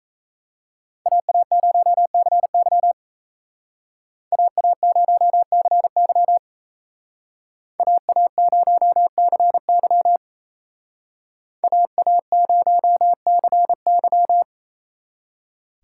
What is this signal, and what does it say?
Morse code, heard as a keyed tone, about 36 wpm, 700 Hz. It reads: AA0CY AA0CY AA0CY AA0CY